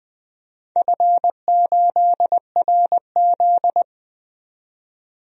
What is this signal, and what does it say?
Morse code, heard as a keyed tone, about 20 wpm, 700 Hz. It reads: F8RZ
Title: DXCC-100 Call Signs - 4 Characters - 20wpm